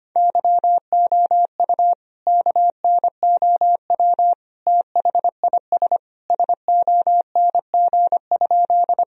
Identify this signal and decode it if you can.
Morse code, heard as a keyed tone, about 25 wpm, 700 Hz. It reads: YOU KNOW THIS SONG?
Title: Sentences from Top 1000 Words - Review Part 2 - 25wpm